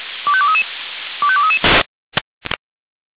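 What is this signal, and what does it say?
Five-tone selective call (ZVEI), heard as quick runs of five tones, tones of about 70 ms. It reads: 25233 25233
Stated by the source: ZVEI five-tone call received on SDR.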